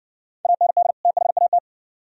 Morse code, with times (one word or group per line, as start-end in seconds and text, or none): 0.4-0.9 AND
1.0-1.6 THAT